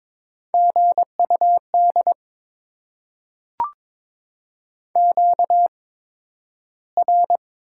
Morse code, heard as a keyed tone, about 22 wpm, 700 Hz. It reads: G U D Q R
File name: Top 20 QSO Elements - 8x Character Spacing - 22wpm